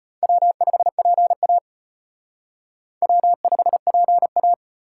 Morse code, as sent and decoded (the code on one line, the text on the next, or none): .-- ..... .--. .- / .-- ..... .--. .-
W5PA W5PA